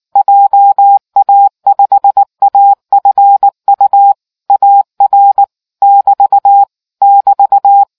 Morse code, heard as a keyed tone, about 19 wpm, 800 Hz. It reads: JA5AFU AR <BT> <BT>